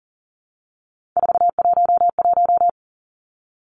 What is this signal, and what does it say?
Morse code, heard as a keyed tone, about 40 wpm, 700 Hz. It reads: 411
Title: Numbers - 3-Digit 40wpm